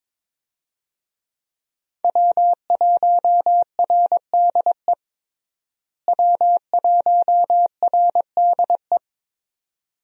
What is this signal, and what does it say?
Morse code, heard as a keyed tone, about 21 wpm, 700 Hz. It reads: W1RDE W1RDE
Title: USA Call Signs - 5 Characters - Repeated 2x - 22wpm